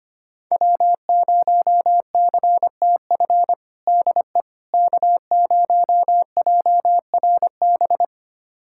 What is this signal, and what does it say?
Morse code, heard as a keyed tone, about 25 wpm, 700 Hz. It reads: W0CTF DE K0JRB
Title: USA Callsign DE Callsign - 5 Characters - 25wpm